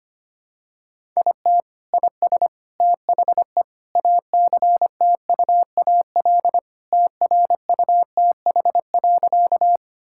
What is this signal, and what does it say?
Morse code, heard as a keyed tone, about 25 wpm, 700 Hz. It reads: IT IS THE ACTUAL TRUTH.
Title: Sentences from Top 1000 Words - Review Part 5 - 25wpm